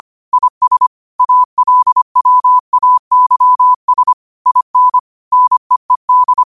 Morse code, heard as a keyed tone, about 25 wpm, 1000 Hz. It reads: IS ALWAYS IN NEED
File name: qotdfiles / 25wpm